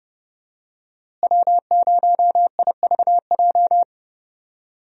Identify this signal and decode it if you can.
Morse code, heard as a keyed tone, about 31 wpm, 700 Hz. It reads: W0IVJ
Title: USA Call Signs - 5 Characters - Repeated 2x - 30wpm